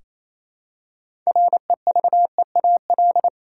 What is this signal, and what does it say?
Morse code, heard as a keyed tone, about 28 wpm, 700 Hz. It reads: REVEAL
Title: Six Letter Words - 28wpm